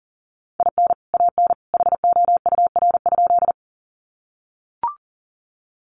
Morse code, heard as a keyed tone, about 40 wpm, 700 Hz. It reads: IN AN HOUR?
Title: Sentences from Top 400 Words 40wpm